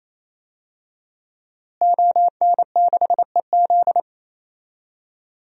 Morse code, heard as a keyed tone, about 28 wpm, 700 Hz. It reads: ON6EZ
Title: DXCC-100 Call Signs - 5 Characters - Repeated 3x - 28wpm